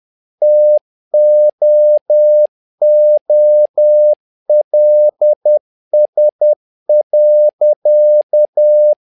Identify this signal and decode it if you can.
Morse code, heard as a keyed tone, about 10 wpm, 600 Hz. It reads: TOOLS.
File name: learning center / tower trouble / tower trouble 10wpm